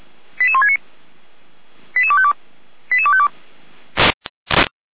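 Five-tone selective call (ZVEI), heard as quick runs of five tones, tones of about 70 ms. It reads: 88169 88252 88252
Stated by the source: ZVEI five-tone call received by SDR.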